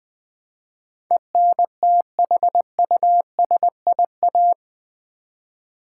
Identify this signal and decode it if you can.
Morse code, heard as a keyed tone, about 21 wpm, 700 Hz. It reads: ENTHUSIA
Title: Build A Word - 4 Letters At A Time - 12-Letter Words - 20wpm